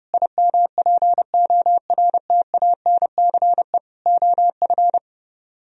Morse code, heard as a keyed tone, about 33 wpm, 700 Hz. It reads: IMPORTANCE OF